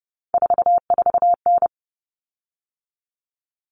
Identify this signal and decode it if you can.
Morse code, heard as a keyed tone, about 30 wpm, 700 Hz. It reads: 44N
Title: Common Cut Numbers - 3-Digit 30wpm